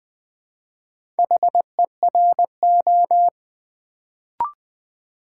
Morse code, heard as a keyed tone, about 20 wpm, 700 Hz. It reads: HERO
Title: Build A Word - 1 Letter At A Time - 4-Letter Words - 20wpm